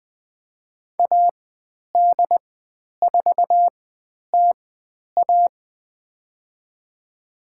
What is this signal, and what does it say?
Morse code, heard as a keyed tone, about 20 wpm, spaced as for 10 wpm, 700 Hz. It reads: AD4TA
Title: 109 Lesson 18g - Letter m - Call Signs 20wpm